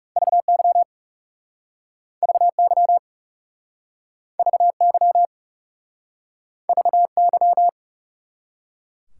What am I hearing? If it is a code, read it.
Morse code, heard as a keyed tone, about 40 wpm, 700 Hz. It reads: VY VY VY VY